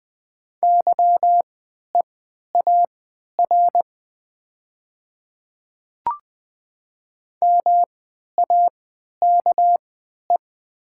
Morse code, heard as a keyed tone, about 20 wpm, spaced as for 11 wpm, 700 Hz. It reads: YEAR MAKE
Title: 3x Character Spacing - Top 500 Words - 20wpm